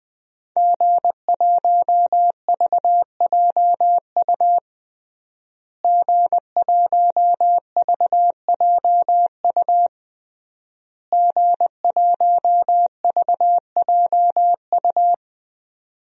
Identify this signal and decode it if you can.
Morse code, heard as a keyed tone, about 21 wpm, 700 Hz. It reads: G1VJU G1VJU G1VJU
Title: DXCC-100 Call Signs - 5 Characters - Repeated 3x - 20wpm